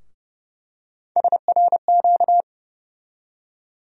Morse code, heard as a keyed tone, about 32 wpm, 700 Hz. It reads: SRQ
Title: USA Call Sign Postfixes - 3 Letters 30wpm